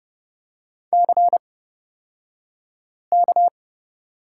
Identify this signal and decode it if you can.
Morse code, heard as a keyed tone, about 30 wpm, spaced as for 5 wpm, 700 Hz. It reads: CK
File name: Lesson 37d - Letter x - Two Characters 30wpm